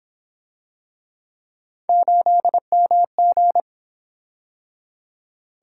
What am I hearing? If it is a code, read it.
Morse code, heard as a keyed tone, about 29 wpm, 700 Hz. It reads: 8MG